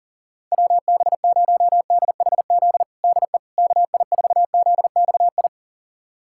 Morse code, heard as a keyed tone, about 40 wpm, 700 Hz. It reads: WB0DHZ DE KI4ZXI
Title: USA Callsign DE Callsign - 6 Characters - 40wpm